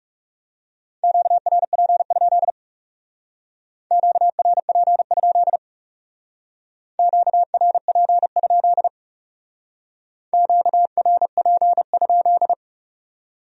Morse code, heard as a keyed tone, about 38 wpm, 700 Hz. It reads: QRP? QRP? QRP? QRP?